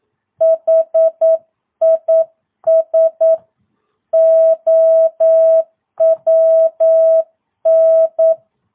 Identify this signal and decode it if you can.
Morse code, heard as a keyed tone, about 9 wpm, 650 Hz. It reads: HIS OWN
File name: jh8cbh / 2022cw